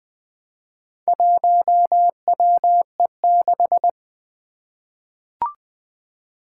Morse code, heard as a keyed tone, about 20 wpm, 700 Hz. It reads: 1WE6
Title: Four Challenging Letter-Number Combinations 20wpm